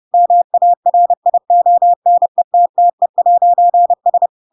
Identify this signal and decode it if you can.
Morse code, heard as a keyed tone, about 30 wpm, 700 Hz. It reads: MARIONETTE'S
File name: Pinocchio / Pinocchio ch5 30 30 ik1hge